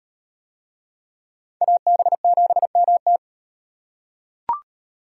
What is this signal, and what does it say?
Morse code, heard as a keyed tone, about 36 wpm, 700 Hz. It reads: AB7MT